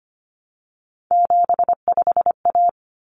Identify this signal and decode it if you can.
Morse code, heard as a keyed tone, about 25 wpm, 700 Hz. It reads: 75A
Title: Common Cut Numbers - 3-Digit 25wpm